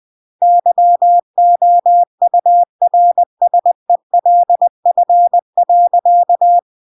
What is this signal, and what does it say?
Morse code, heard as a keyed tone, about 20 wpm, 700 Hz. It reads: YOURSELF.
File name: CW / jodorowsky / 82 commandments 20-20